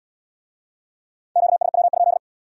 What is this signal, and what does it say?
Morse code, heard as a keyed tone, about 68 wpm, 700 Hz. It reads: QSW?